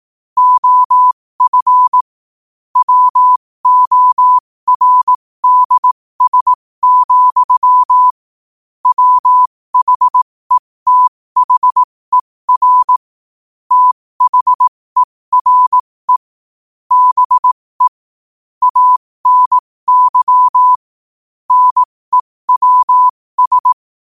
Morse code, heard as a keyed tone, about 18 wpm, spaced as for 15 wpm, 1000 Hz. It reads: OF WORDS, WHETHER THERE BE ANY NEWS
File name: qotdfiles / 15wpm